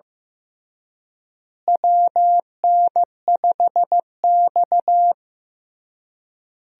Morse code, heard as a keyed tone, about 15 wpm, 700 Hz. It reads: WN5X
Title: USA Call Signs - 4 Characters - Repeated 3x - 15wpm